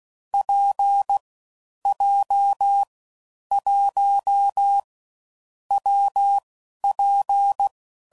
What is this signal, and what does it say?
Morse code, heard as a keyed tone, about 16 wpm, 800 Hz. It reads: P J 1 W P
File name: Combinations / Difficult Combos / WPJ WPJ1